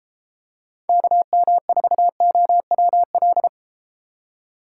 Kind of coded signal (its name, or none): Morse code